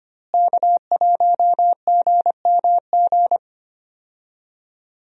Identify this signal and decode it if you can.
Morse code, heard as a keyed tone, about 25 wpm, 700 Hz. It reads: K1GMG
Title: USA Call Signs - 5 Characters - Repeated 2x - 25wpm